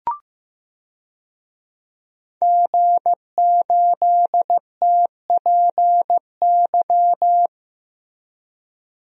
Morse code, heard as a keyed tone, about 15 wpm, 700 Hz. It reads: G8TPY
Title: DXCC-100 Call Signs - 5 Characters - Repeated 3x - 15wpm